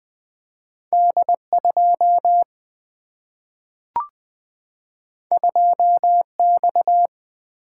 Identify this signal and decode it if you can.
Morse code, heard as a keyed tone, about 20 wpm, 700 Hz. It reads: D 2 2 X
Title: Single-Letter Single-Number 20wpm